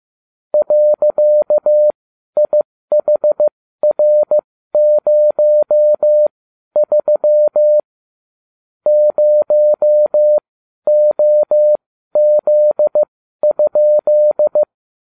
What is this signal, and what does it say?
Morse code, heard as a keyed tone, about 15 wpm, spaced as for 12 wpm, 600 Hz. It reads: .IHR03 0OZ?